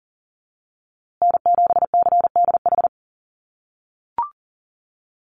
Morse code, heard as a keyed tone, about 40 wpm, 700 Hz. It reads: N7CDH
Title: USA Call Signs Repeated 3x 40wpm